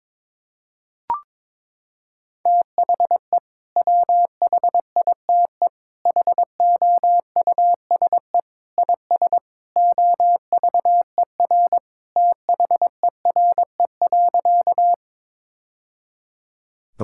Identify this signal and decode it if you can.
Morse code, heard as a keyed tone, about 21 wpm, 700 Hz. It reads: THE WHITE HOUSE IS OVER THERE.
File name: practice / 22wpm / Sentences with 3-Word Phrases - 22wpm